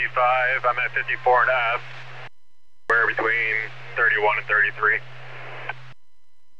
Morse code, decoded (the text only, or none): none